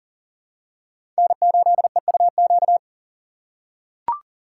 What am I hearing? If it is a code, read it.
Morse code, heard as a keyed tone, about 37 wpm, 700 Hz. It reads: N8EUQ
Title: USA Call Signs - 5 Characters - 6x Character Spacing - 40wpm